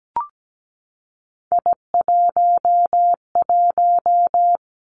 Morse code, heard as a keyed tone, about 17 wpm, 700 Hz. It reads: I11
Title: Three Challenging Letter-Number Combinations 17wpm